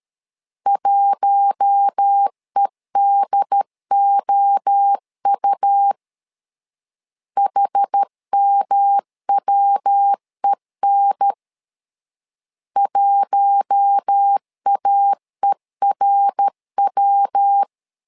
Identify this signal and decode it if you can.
Morse code, heard as a keyed tone, about 13 wpm, 800 Hz. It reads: 1EDOU HMWEN 1AERW
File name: LEZIONI / 4 Q1 12wpm 2028